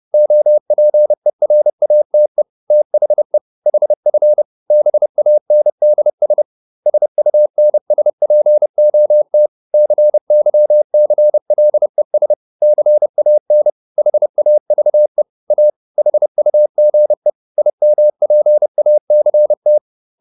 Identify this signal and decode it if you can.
Morse code, heard as a keyed tone, about 30 wpm, 600 Hz. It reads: OPERATE THE HF BANDS SUNSPOT CYCLES CAN HAVE A HUGE IMPACT